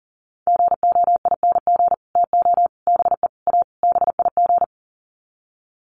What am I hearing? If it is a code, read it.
Morse code, heard as a keyed tone, about 40 wpm, 700 Hz. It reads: GOING TO BE A BIG
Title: Sets of 5 Words Encore2 40wpm